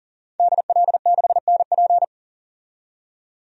Morse code, heard as a keyed tone, about 40 wpm, 700 Hz. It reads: DL6NP
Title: DXCC-100 Call Signs - 5 Characters - 40wpm